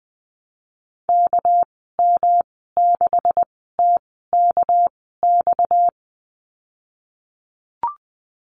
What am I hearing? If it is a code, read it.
Morse code, heard as a keyed tone, about 20 wpm, spaced as for 14 wpm, 700 Hz. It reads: KM6TKX
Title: 245 Lesson 39g - Letter z - Call Signs 20wpm